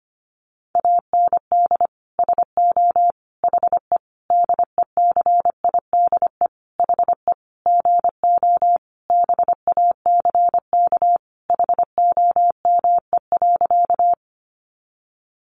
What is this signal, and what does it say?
Morse code, heard as a keyed tone, about 25 wpm, 700 Hz. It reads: AND SO HE DECIDE HE GO BACK HOME.
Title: Sentences from Top 500 Words 25wpm